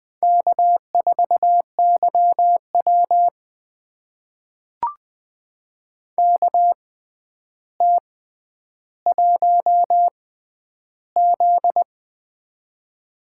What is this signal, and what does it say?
Morse code, heard as a keyed tone, about 20 wpm, spaced as for 10 wpm, 700 Hz. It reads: K4YW KT1Z
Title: USA Call Signs - 4 Characters - 6x Character Spacing - 20wpm